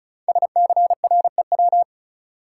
Morse code, heard as a keyed tone, about 36 wpm, 700 Hz. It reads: SCREW
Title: Sentences from Top 1600 Words - New Word Drill Alt - 35wpm